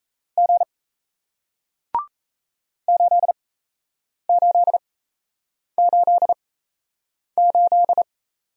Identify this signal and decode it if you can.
Morse code, heard as a keyed tone, about 37 wpm, spaced as for 8 wpm, 700 Hz. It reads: G 8888